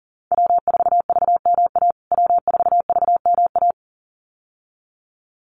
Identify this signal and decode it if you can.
Morse code, heard as a keyed tone, about 40 wpm, 700 Hz. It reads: W4VMA W4VMA
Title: USA Call Signs Repeated 2x 40wpm